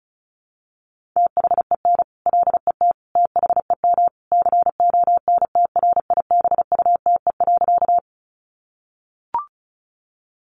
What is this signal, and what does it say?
Morse code, heard as a keyed tone, about 34 wpm, 700 Hz. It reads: THEN LET THEM CONTRIBUTE.
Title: Sentences from Top 1400 Words 35wpm